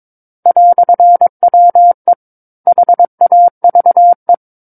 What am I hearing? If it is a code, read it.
Morse code, heard as a keyed tone, about 22 wpm, 700 Hz. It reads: "WE HAVE
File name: Book One / 13-Chapter 13 - 22 WPM